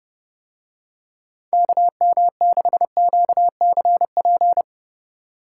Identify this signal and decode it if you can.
Morse code, heard as a keyed tone, about 30 wpm, 700 Hz. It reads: KM6QCP